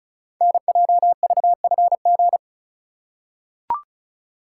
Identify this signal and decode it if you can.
Morse code, heard as a keyed tone, about 36 wpm, 700 Hz. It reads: NJVFG
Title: Code Groups - 5 Letters - 35wpm